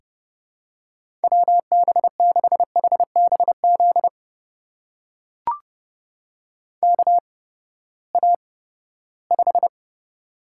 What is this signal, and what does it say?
Morse code, heard as a keyed tone, about 32 wpm, spaced as for 17 wpm, 700 Hz. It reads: WB6HBZ KA5
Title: USA Call Signs - 6 Characters - 8x Character Spacing - 30wpm